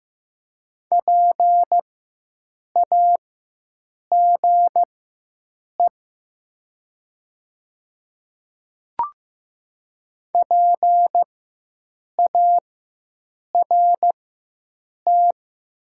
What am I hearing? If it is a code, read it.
Morse code, heard as a keyed tone, about 15 wpm, spaced as for 7 wpm, 700 Hz. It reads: PAGE PART
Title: 4x Character Spacing - Top 300 Words - 15wpm